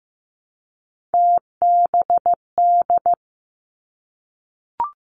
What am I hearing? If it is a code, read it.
Morse code, heard as a keyed tone, about 15 wpm, 700 Hz. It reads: TBD
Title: Three Challenging Letter-Number Combinations 15wpm